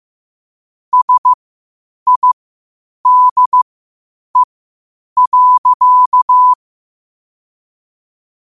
Morse code, heard as a keyed tone, about 15 wpm, spaced as for 8 wpm, 1000 Hz. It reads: SIDE.
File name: qotdfiles / 8wpm